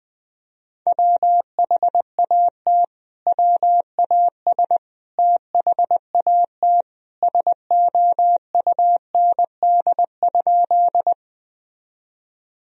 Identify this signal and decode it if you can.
Morse code, heard as a keyed tone, about 20 wpm, 700 Hz. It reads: WHAT WAS THAT SOUND?
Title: Sentences from Top 1000 Words - Review Part 1 - 20wpm